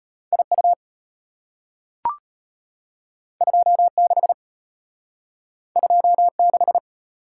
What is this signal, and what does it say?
Morse code, heard as a keyed tone, about 32 wpm, 700 Hz. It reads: IU 26 26